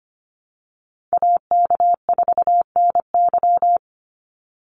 Morse code, heard as a keyed tone, about 25 wpm, 700 Hz. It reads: AK4NY
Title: Call Signs - 25wpm